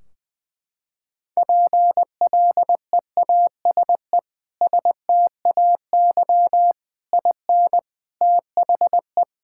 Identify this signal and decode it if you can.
Morse code, heard as a keyed tone, about 20 wpm, 700 Hz. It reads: PLEASE STAY IN THE
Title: Sentences from Top 1600 Words 20wpm